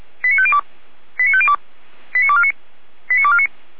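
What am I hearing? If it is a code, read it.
Five-tone selective call (ZVEI), heard as quick runs of five tones, tones of about 70 ms. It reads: 89502 89502 89249 89249
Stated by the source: ZVEI five-tone selective call received on SDR.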